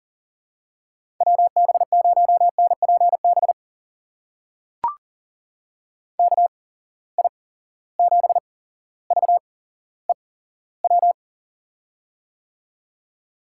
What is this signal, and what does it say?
Morse code, heard as a keyed tone, about 40 wpm, spaced as for 17 wpm, 700 Hz. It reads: WB0NPB KI7VEW
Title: USA Call Signs - 6 Characters - 8x Character Spacing - 40wpm